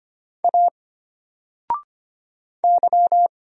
Morse code, heard as a keyed tone, about 25 wpm, 700 Hz. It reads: A Y
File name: Single-Letter-Number-25wpm